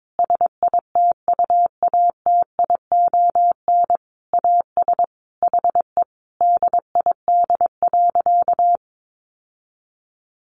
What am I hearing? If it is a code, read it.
Morse code, heard as a keyed tone, about 22 wpm, 700 Hz. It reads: SITUATION AS HE DID.
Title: Sentences from Top 1200 Words 22wpm